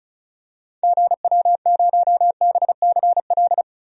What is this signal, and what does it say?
Morse code, heard as a keyed tone, about 36 wpm, 700 Hz. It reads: GW0BCL
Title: DXCC-100 Call Signs - 6 Characters - 35wpm